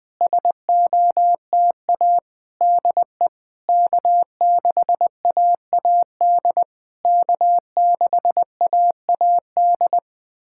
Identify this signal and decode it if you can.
Morse code, heard as a keyed tone, about 20 wpm, 690 Hz. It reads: SOTA DE K6AAD K6AAD
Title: SOTA QSO Practice 20wpm